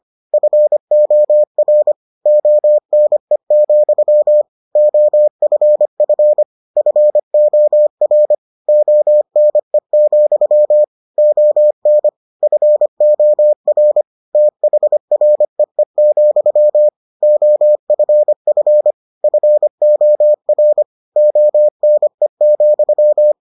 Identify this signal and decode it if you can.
Morse code, heard as a keyed tone, about 25 wpm, 600 Hz. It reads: FOR ONE, OFF FOR ONE, ON FOR THREE, OFF FOR ONE,